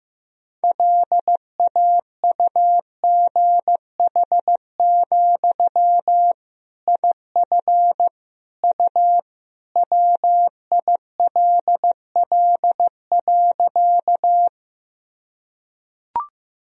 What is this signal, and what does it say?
Morse code, heard as a keyed tone, about 15 wpm, 700 Hz. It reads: LAUGH, IF U WILL.
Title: Sentences from Top 500 Words and Abbreviations 15wpm